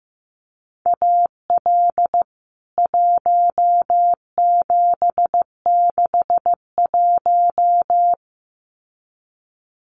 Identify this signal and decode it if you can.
Morse code, heard as a keyed tone, about 15 wpm, 700 Hz. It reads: AL 1761
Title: CWT Exchange 15wpm